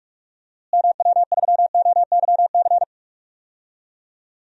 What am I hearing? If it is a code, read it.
Morse code, heard as a keyed tone, about 46 wpm, 700 Hz. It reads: MW3OYC
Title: DXCC-100 Call Signs - 6 Characters - Repeated 2x - 45wpm